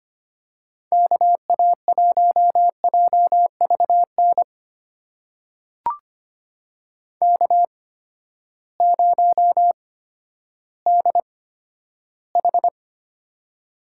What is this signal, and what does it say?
Morse code, heard as a keyed tone, about 25 wpm, spaced as for 12 wpm, 700 Hz. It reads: KA1JVN K0DH
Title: USA Call Signs - 8x Character Spacing - 25wpm